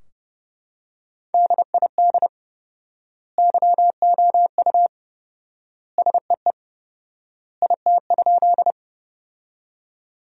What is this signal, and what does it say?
Morse code, heard as a keyed tone, about 32 wpm, 700 Hz. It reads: DID YOU SEE IT?
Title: 4x Word-Spacing - Sentences from Top 100 Words - 30wpm